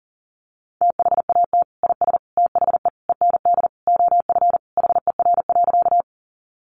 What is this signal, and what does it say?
Morse code, heard as a keyed tone, about 40 wpm, 700 Hz. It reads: THAT IS THE END OF HER.
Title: Sentences from Top 200 Words 40wpm